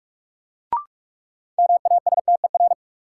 Morse code, heard as a keyed tone, about 46 wpm, 700 Hz. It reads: MASTER
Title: Sentences from Top 1300 Words - New Word Drill Alt - 45wpm